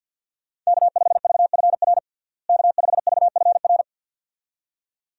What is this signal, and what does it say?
Morse code, heard as a keyed tone, about 51 wpm, 700 Hz. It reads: K5VFR K5VFR
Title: USA Call Signs - 5 Characters - Repeated 2x - 50wpm